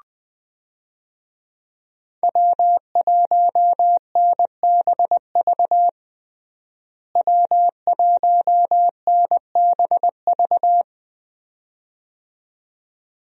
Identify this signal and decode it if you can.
Morse code, heard as a keyed tone, about 21 wpm, 700 Hz. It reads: W1NBV W1NBV
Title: USA Call Signs - 5 Characters - Repeated 2x - 20wpm